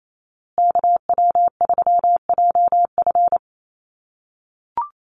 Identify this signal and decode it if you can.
Morse code, heard as a keyed tone, about 28 wpm, 700 Hz. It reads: KW3JF
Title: USA Call Signs Repeated 3x 28wpm